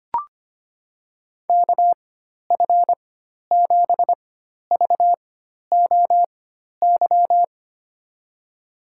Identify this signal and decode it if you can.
Morse code, heard as a keyed tone, about 25 wpm, spaced as for 12 wpm, 700 Hz. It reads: KF7VOY
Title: USA Call Signs - 6 Characters - 4x Character Spacing - 25wpm